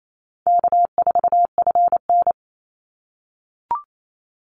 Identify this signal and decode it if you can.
Morse code, heard as a keyed tone, about 28 wpm, 700 Hz. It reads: K4FN
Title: Call Signs - 28wpm